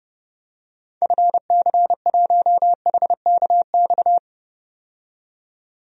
Morse code, heard as a keyed tone, about 30 wpm, 700 Hz. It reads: FC1HKX
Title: DXCC-100 Call Signs - 6 Characters - 30wpm